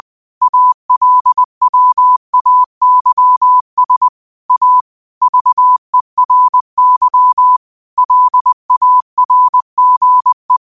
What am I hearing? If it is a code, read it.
Morse code, heard as a keyed tone, about 20 wpm, 1000 Hz. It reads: ALWAYS A VERY LARGE
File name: qotdfiles / 20wpm